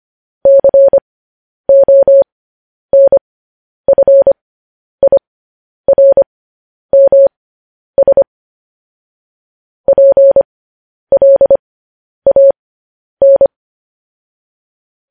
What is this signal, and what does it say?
Morse code, heard as a keyed tone, about 25 wpm, spaced as for 10 wpm, 550 Hz. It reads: CONFIRMS PLAN
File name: podcast / News headlines-1x-25x10